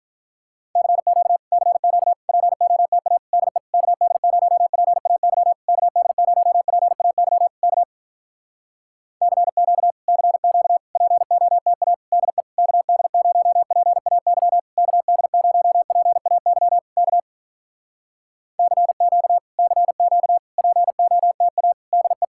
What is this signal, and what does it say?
Morse code, heard as a keyed tone, about 46 wpm, 700 Hz. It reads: CQ CQ POTA DE KD0PAY KD0PAY K CQ CQ POTA DE KD0PAY KD0PAY K CQ CQ POTA DE